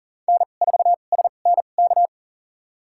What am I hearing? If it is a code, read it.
Morse code, heard as a keyed tone, about 40 wpm, spaced as for 29 wpm, 700 Hz. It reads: N4SNK
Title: USA Call Signs - 5 Characters - 2x Character Spacing - 40wpm